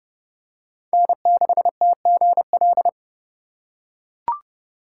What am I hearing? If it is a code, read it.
Morse code, heard as a keyed tone, about 28 wpm, 700 Hz. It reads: N6TGL